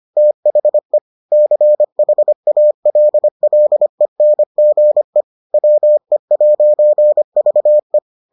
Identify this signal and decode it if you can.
Morse code, heard as a keyed tone, about 25 wpm, 600 Hz. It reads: THE CHALLENGE WE'VE